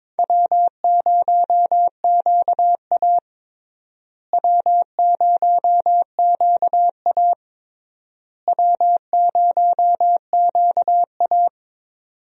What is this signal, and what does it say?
Morse code, heard as a keyed tone, about 22 wpm, 700 Hz. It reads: W0QA W0QA W0QA
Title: USA Call Signs - 4 Characters - Repeated 3x - 22wpm